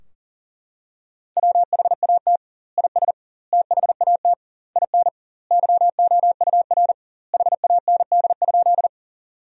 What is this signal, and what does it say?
Morse code, heard as a keyed tone, about 40 wpm, 700 Hz. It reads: WHAT IS THAT IN YOUR HAND?
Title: 2x Word-Spacing - Sentences from Top 200 Words - 40wpm